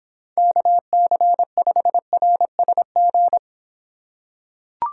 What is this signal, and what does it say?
Morse code, heard as a keyed tone, about 25 wpm, 700 Hz. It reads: KC5RSG